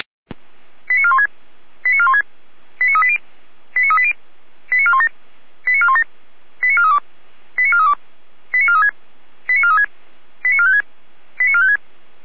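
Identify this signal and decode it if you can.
Five-tone selective call (ZVEI), heard as quick runs of five tones, tones of about 70 ms. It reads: 89416 89416 89380 89380 89417 89417 89432 89432 89436 89436 89456 89456
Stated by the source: ZVEI five-tone call received by SDR.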